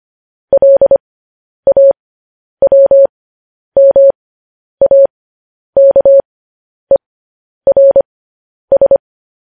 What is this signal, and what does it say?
Morse code, heard as a keyed tone, about 25 wpm, spaced as for 10 wpm, 550 Hz. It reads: LAWMAKERS
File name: podcast / News headlines-1x-25x10